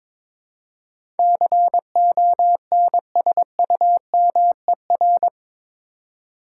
Morse code, heard as a keyed tone, about 22 wpm, 700 Hz. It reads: CONSUMER
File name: practice / 22wpm / Eight Letter Words - 22wpm